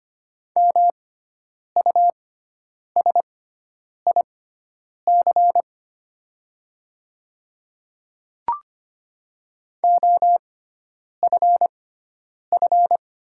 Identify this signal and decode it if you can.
Morse code, heard as a keyed tone, about 25 wpm, spaced as for 9 wpm, 700 Hz. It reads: MUSIC OFF
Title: 6x Character Spacing - Top 300 Words - 25wpm